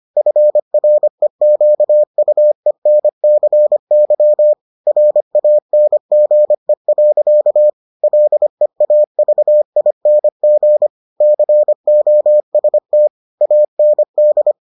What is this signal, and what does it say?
Morse code, heard as a keyed tone, about 25 wpm, 600 Hz. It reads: FREQUENCY RANGE. LEAVING COST AND